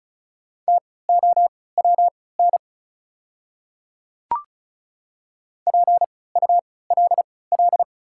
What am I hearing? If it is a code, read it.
Morse code, heard as a keyed tone, about 36 wpm, spaced as for 20 wpm, 700 Hz. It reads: TOWN PULL
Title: 3x Character Spacing - Top 500 Words - 35wpm